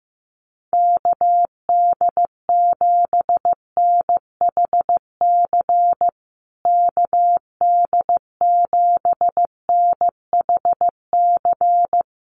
Morse code, heard as a keyed tone, about 15 wpm, 700 Hz. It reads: KD7NHC KD7NHC
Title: USA Call Signs Repeated 3x 15wpm